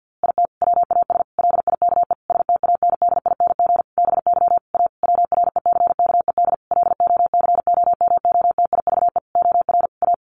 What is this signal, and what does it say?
Morse code, heard as a keyed tone, about 50 wpm, 700 Hz. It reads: IT WAS LIKE STANDING BY A WRECKED LOCOMOTIVE OR A
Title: A Sound Of Thunder 50wpm